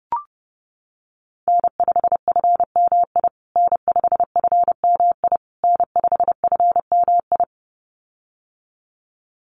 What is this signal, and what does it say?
Morse code, heard as a keyed tone, about 30 wpm, 700 Hz. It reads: N5FMI N5FMI N5FMI
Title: USA Call Signs Repeated 3x 30wpm